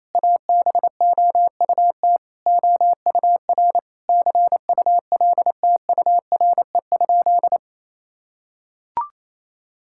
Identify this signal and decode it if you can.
Morse code, heard as a keyed tone, about 28 wpm, 700 Hz. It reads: ABOUT OUR CULTURE?
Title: Sentences from Top 1000 Words - Review Part 6 - 28wpm